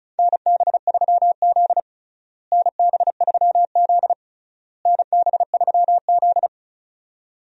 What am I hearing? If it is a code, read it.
Morse code, heard as a keyed tone, about 34 wpm, 700 Hz. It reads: NB3Z NB3Z NB3Z
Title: USA Call Signs - 4 Characters - Repeated 3x - 35wpm